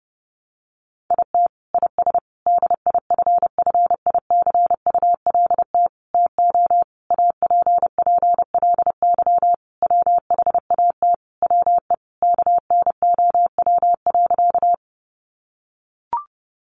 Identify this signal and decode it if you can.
Morse code, heard as a keyed tone, about 28 wpm, 700 Hz. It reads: IT IS DIFFICULT TO APPLY WHAT WE KNOW.
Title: Sentences from Top 1200 Words 30wpm